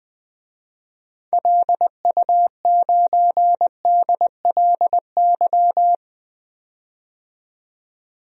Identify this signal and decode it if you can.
Morse code, heard as a keyed tone, about 20 wpm, 700 Hz. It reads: LU9DLY
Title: DXCC-100 Call Signs - 6 Characters - 20wpm